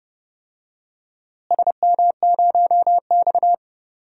Morse code, heard as a keyed tone, about 31 wpm, 700 Hz. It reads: SM0X
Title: DXCC-100 Call Signs - 4 Characters - Repeated 3x - 30wpm